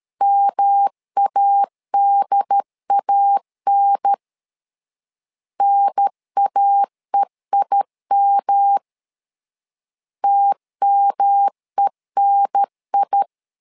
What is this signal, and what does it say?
Morse code, heard as a keyed tone, about 13 wpm, 800 Hz. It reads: MADAN NAEIM TMENI
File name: LEZIONI / 3 MWRO / 3 M 12wpm 1541